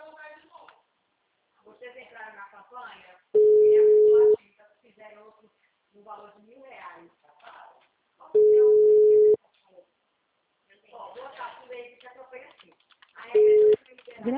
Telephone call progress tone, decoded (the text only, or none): ringback tone